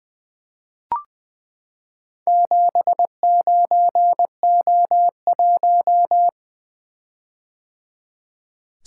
Morse code, heard as a keyed tone, about 20 wpm, 700 Hz. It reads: 79O1
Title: Four Letter-Number Combinations - 20wpm